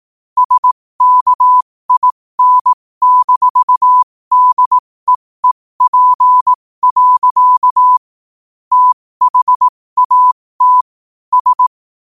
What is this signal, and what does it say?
Morse code, heard as a keyed tone, about 18 wpm, spaced as for 15 wpm, 1000 Hz. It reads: SKIN-DEEP. THAT S
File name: qotdfiles / 15wpm